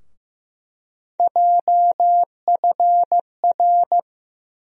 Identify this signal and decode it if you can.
Morse code, heard as a keyed tone, about 15 wpm, 700 Hz. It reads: JFR
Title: USA Call Sign Postfixes - 3 Letters 15wpm